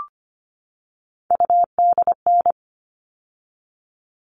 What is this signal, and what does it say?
Morse code, heard as a keyed tone, about 25 wpm, 700 Hz. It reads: UDN
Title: All Cut Numbers - 3-Digit 25wpm